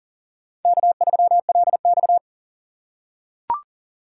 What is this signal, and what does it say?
Morse code, heard as a keyed tone, about 37 wpm, 700 Hz. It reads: K3LX
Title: USA Call Signs - 4 Characters - 6x Character Spacing - 40wpm